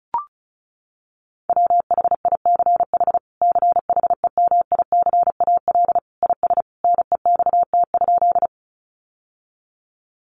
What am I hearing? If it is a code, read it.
Morse code, heard as a keyed tone, about 34 wpm, 700 Hz. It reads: WHICH CHEMICAL IS NEXT?
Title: Sentences from Top 1200 Words 35wpm